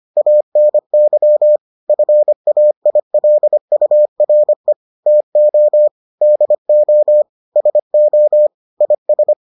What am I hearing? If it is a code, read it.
Morse code, heard as a keyed tone, about 25 wpm, 600 Hz. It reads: ANY FAILURE TO DO SO IS